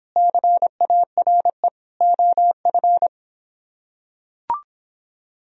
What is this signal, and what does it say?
Morse code, heard as a keyed tone, about 24 wpm, 700 Hz. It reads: CARE OF